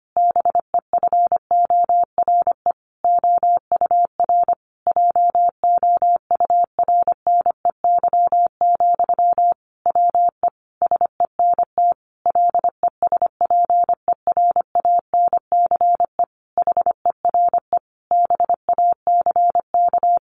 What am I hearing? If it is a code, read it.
Morse code, heard as a keyed tone, about 25 wpm, 700 Hz. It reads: BEFORE OUR JOURNEY, WE SENT LESPERANCE HERE BACK
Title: A Sound Of Thunder 25wpm